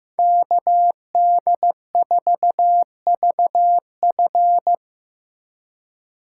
Morse code, heard as a keyed tone, about 18 wpm, 700 Hz. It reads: KD4VF
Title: Speed Racing - USA 2x2 Call Signs 15wpm